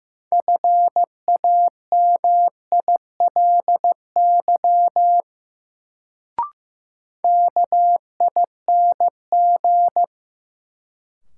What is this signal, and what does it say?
Morse code, heard as a keyed tone, about 15 wpm, 700 Hz. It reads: FAMILY KING
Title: Sentences from Top 400 Words - New Word Drill - 15wpm